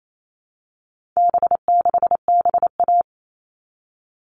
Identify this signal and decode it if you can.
Morse code, heard as a keyed tone, about 28 wpm, 700 Hz. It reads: B6BA
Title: All Cut Numbers - 4-Digit 28wpm